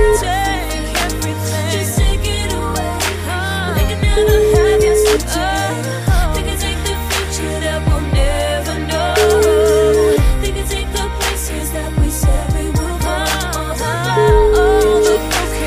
Ringback tone (Europe, 425 Hz), with noise.